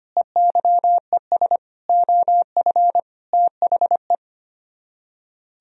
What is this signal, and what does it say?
Morse code, heard as a keyed tone, about 29 wpm, 700 Hz. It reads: EYES OF THE
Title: Speed-Racing - Sets of 5 Words 25wpm